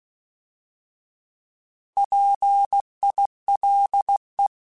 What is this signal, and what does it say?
Morse code, heard as a keyed tone, about 16 wpm, 800 Hz. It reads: PILE
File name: Alphabetical / Alphabetical mp3s / Ps @15